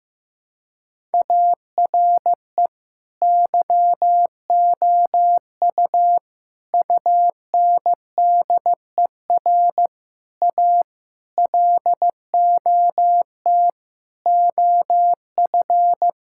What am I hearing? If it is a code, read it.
Morse code, heard as a keyed tone, about 15 wpm, 700 Hz. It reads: ARE YOU UNDER A LOT OF
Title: Sentences from Top 1000 Words - Review Part 6 - 15wpm